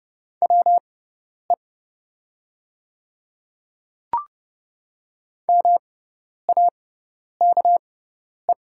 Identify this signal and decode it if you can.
Morse code, heard as a keyed tone, about 30 wpm, spaced as for 10 wpm, 700 Hz. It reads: WE MAKE
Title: 6x Character Spacing - Top 100 Words - 30wpm